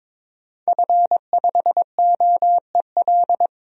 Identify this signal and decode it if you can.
Morse code, heard as a keyed tone, about 22 wpm, 700 Hz. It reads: F5OEL
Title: DXCC-100 Call Signs - 5 Characters - Repeated 2x - 22wpm